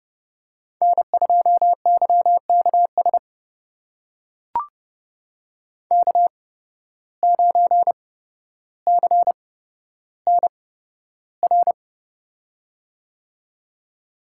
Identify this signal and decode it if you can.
Morse code, heard as a keyed tone, about 30 wpm, spaced as for 13 wpm, 700 Hz. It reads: N2YKS K9CNR
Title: USA Call Signs - 5 Characters - 8x Character Spacing - 30wpm